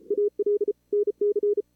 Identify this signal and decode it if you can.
Morse code, heard as a keyed tone, about 36 wpm, 440 Hz. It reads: AL NC
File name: CWSprint / Feb2022 / k5tr-feb-na-sprint-cw-k5ot -2300